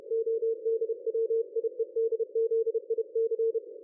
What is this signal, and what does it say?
Morse code, heard as a keyed tone, about 30 wpm, 450 Hz. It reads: ODWIEDZIC
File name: dlazaawansowanych / nagraniazpasma / ExpertSDR2 line out 23.01.2020 14.22.53